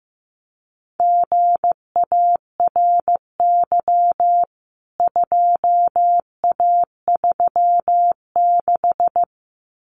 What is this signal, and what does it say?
Morse code, heard as a keyed tone, about 15 wpm, 700 Hz. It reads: GARY 2A36
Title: CWT Exchange 15wpm